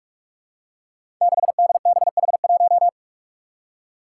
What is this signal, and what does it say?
Morse code, heard as a keyed tone, about 44 wpm, 700 Hz. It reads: 6DBH1
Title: Five Challenging Letter-Number Combinations 45wpm